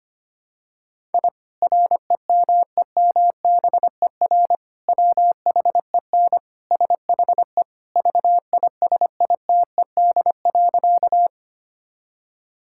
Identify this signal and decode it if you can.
Morse code, heard as a keyed tone, about 25 wpm, 700 Hz. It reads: I REMEMBER WHEN SHE VISITED.
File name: practice / 25wpm / Sentences with 2-Word Phrases - 25wpm